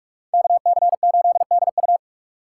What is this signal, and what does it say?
Morse code, heard as a keyed tone, about 42 wpm, 700 Hz. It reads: KC8DU